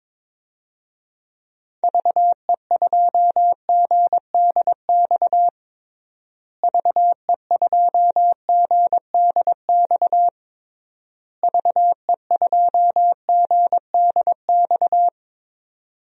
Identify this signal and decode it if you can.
Morse code, heard as a keyed tone, about 22 wpm, 700 Hz. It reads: VE2GDX VE2GDX VE2GDX
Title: DXCC-100 Call Signs - 6 Characters - Repeated 3x - 22wpm